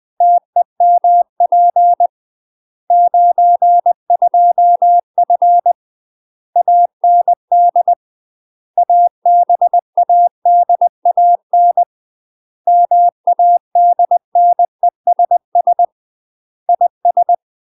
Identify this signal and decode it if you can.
Morse code, heard as a keyed tone, about 20 wpm, 700 Hz. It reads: TEMP 92F AND ABADAN MADNESS IS
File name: UK MASTER QSOs TRAINING / 20WPM / UK MASTER QSOs 0034